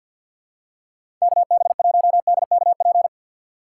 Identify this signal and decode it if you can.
Morse code, heard as a keyed tone, about 53 wpm, 700 Hz. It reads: KB1DKP
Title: USA Call Signs - 6x Character Spacing - 50wpm